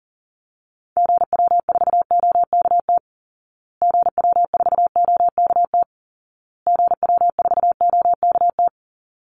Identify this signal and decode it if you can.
Morse code, heard as a keyed tone, about 40 wpm, 700 Hz. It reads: GW4OKT GW4OKT GW4OKT
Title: Call Signs Repeated 2x - 40wpm